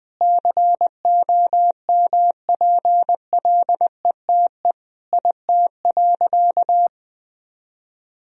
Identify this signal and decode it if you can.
Morse code, heard as a keyed tone, about 20 wpm, 700 Hz. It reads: COMPLETE IT.
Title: Sentences from Top 1000 Words - Review Part 3 - 20wpm